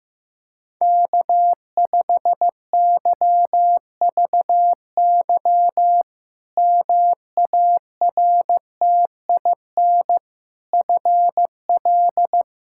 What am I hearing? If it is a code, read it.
Morse code, heard as a keyed tone, about 15 wpm, 700 Hz. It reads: K5YVY MARTIN FL
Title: K1USN SST QSO Practice - 15wpm